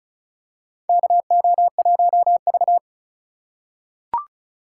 Morse code, heard as a keyed tone, about 34 wpm, 700 Hz. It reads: KO1V
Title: USA Call Signs - 4 Characters - 4x Character Spacing - 35wpm